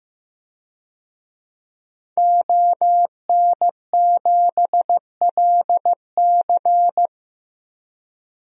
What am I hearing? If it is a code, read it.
Morse code, heard as a keyed tone, about 15 wpm, 700 Hz. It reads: ON7LC
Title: DXCC-100 Call Signs - 5 Characters - 15wpm